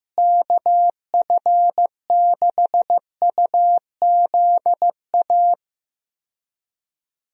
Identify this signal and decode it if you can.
Morse code, heard as a keyed tone, about 15 wpm, 700 Hz. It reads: KF6UZA
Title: USA Call Signs - 6 Characters - Repeated 3x - 15wpm